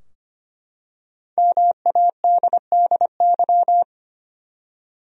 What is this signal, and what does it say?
Morse code, heard as a keyed tone, about 25 wpm, 700 Hz. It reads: MADDY
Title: USA Common Names - 25wpm